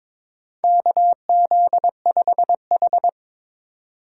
Morse code, heard as a keyed tone, about 21 wpm, 700 Hz. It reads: KZ5H
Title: USA Call Signs - 4 Characters - 2x Character Spacing - 22wpm